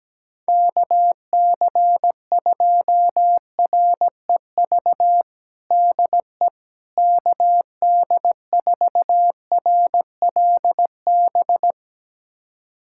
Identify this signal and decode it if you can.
Morse code, heard as a keyed tone, about 17 wpm, 700 Hz. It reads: KC2REV DE KD4RLB
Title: USA Callsign DE Callsign - 6 Characters - 17wpm